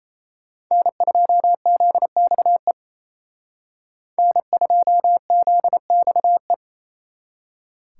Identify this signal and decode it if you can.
Morse code, heard as a keyed tone, about 35 wpm, 700 Hz. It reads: N2ZXE N2ZXE